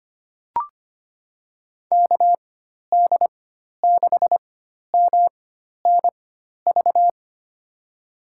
Morse code, heard as a keyed tone, about 25 wpm, spaced as for 12 wpm, 700 Hz. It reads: KD6MNV
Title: USA Call Signs - 6 Characters - 4x Character Spacing - 25wpm